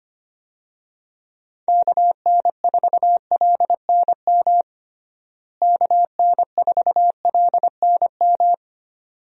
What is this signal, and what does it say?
Morse code, heard as a keyed tone, about 25 wpm, 700 Hz. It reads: KN4LNM KN4LNM
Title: USA Call Signs - 6 Characters - Repeated 2x - 25wpm